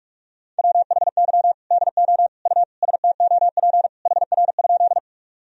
Morse code, heard as a keyed tone, about 46 wpm, 700 Hz. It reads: WHY DO U STOP HR?
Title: Sentences from Top 300 Words and Abbreviations 45wpm